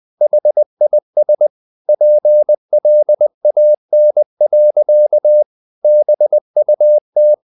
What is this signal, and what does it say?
Morse code, heard as a keyed tone, about 20 wpm, 600 Hz. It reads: HIS PLAN. BUT
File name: the old man and the sea / the old man and the sea part 1 20wpm